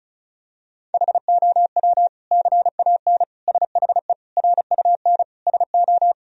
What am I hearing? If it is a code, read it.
Morse code, heard as a keyed tone, about 34 wpm, 700 Hz. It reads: HOW CAN SHE RUN SO
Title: Sentences from Top 1000 Words - Review Part 2 - 35wpm